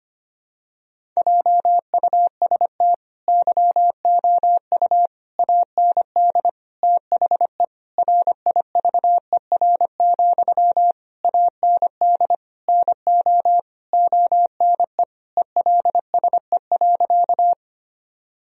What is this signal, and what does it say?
Morse code, heard as a keyed tone, about 25 wpm, 700 Hz. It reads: JUST YOU AND THE RIVER, AND NO ONE ELSE.
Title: Sentences from Top 1000 Words - Review Part 1 - 25wpm